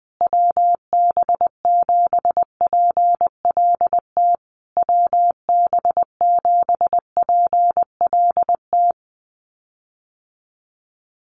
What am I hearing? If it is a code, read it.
Morse code, heard as a keyed tone, about 20 wpm, 700 Hz. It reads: WB7PLT WB7PLT
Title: USA Call Signs Repeated 3x 20wpm